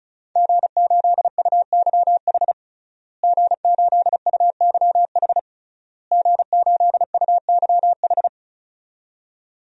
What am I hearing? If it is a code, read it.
Morse code, heard as a keyed tone, about 36 wpm, 700 Hz. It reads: G8UYH G8UYH G8UYH
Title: DXCC-100 Call Signs - 5 Characters - Repeated 3x - 35wpm